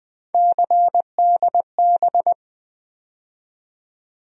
Morse code, heard as a keyed tone, about 20 wpm, 700 Hz. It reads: CDB
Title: USA Call Sign Postfixes - 3 Letters 20wpm